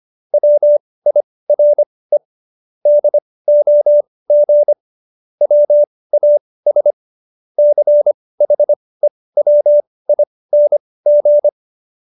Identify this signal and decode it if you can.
Morse code, heard as a keyed tone, about 25 wpm, spaced as for 18 wpm, 590 Hz. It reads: WIRE DOG WAS CHEWING